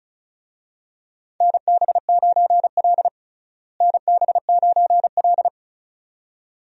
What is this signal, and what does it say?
Morse code, heard as a keyed tone, about 34 wpm, 700 Hz. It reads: NB9L NB9L
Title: USA Call Signs - 4 Characters - Repeated 2x - 35wpm